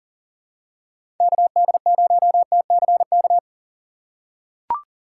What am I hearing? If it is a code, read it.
Morse code, heard as a keyed tone, about 40 wpm, 700 Hz. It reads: KD0TCK
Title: USA Call Signs - 6 Characters - 4x Character Spacing - 40wpm